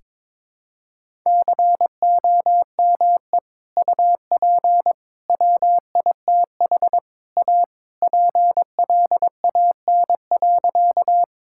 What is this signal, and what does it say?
Morse code, heard as a keyed tone, about 22 wpm, 700 Hz. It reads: COME UP WITH A PLAN.
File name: practice / 22wpm / Sentences with 4-Word Phrases - 22wpm